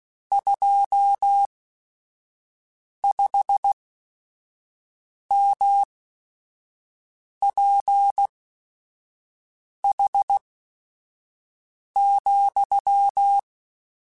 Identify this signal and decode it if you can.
Morse code, heard as a keyed tone, about 16 wpm, spaced as for 5 wpm, 800 Hz. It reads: 25MPH,
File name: Weather Sets / Weather Sets-mp3s / Weather 5 5 @5